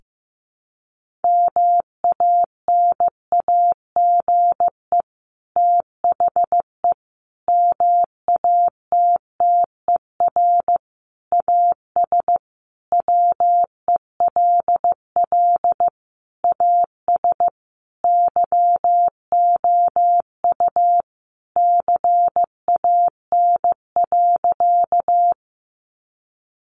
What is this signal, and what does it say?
Morse code, heard as a keyed tone, about 14 wpm, 700 Hz. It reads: MANAGE THE MATTER AS WELL AS YOU CAN.
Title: Sentences from Top 1400 Words 15wpm